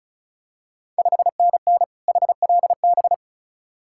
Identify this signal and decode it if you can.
Morse code, heard as a keyed tone, about 34 wpm, 700 Hz. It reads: 5NN HLB
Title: Ontario QSO Party Exchange 35wpm